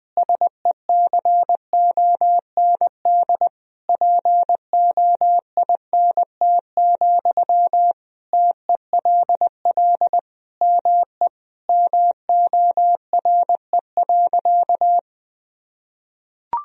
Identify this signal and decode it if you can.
Morse code, heard as a keyed tone, about 21 wpm, 700 Hz. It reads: SECOND POINT, TELL ME MORE.
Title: Sentences from Top 1000 Words - Review Part 1 - 20wpm